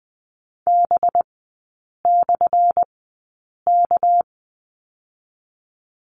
Morse code, heard as a keyed tone, about 20 wpm, spaced as for 8 wpm, 700 Hz. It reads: B/K
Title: 205 Lesson 33e - Letter j - Three Characters 20wpm